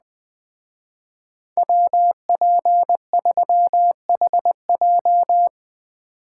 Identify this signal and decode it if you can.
Morse code, heard as a keyed tone, about 19 wpm, 700 Hz. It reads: WP3HJ
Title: Speed Racing - USA 2x2 Call Signs 17wpm